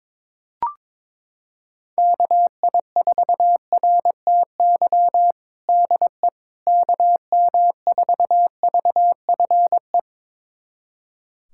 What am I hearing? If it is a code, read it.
Morse code, heard as a keyed tone, about 22 wpm, 700 Hz. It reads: KI4RTY DE KM4VFE
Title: USA Callsign DE Callsign - 6 Characters - 22wpm